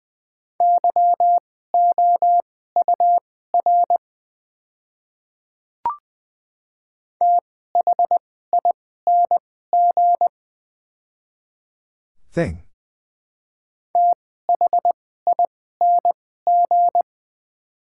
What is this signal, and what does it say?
Morse code, heard as a keyed tone, about 20 wpm, spaced as for 14 wpm, 700 Hz. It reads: YOUR THING THING
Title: 2x Character Spacing - Top 100 Words - 20wpm